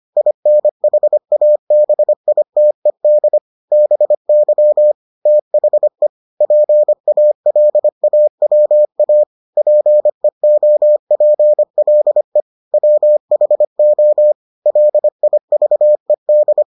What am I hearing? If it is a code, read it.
Morse code, heard as a keyed tone, about 25 wpm, 600 Hz. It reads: INHABITED BY THE PALAWA PEOPLE WHO LIVED